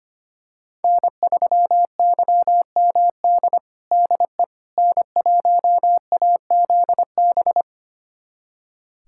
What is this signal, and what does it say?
Morse code, heard as a keyed tone, about 25 wpm, 700 Hz. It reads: N3YMD DE N1AZB
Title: USA Callsign DE Callsign - 5 Characters - 25wpm